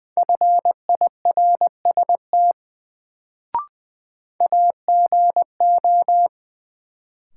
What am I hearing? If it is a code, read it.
Morse code, heard as a keyed tone, about 20 wpm, 700 Hz. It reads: FIRST AGO
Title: Sets of 2 Words - New Word Drill - 20wpm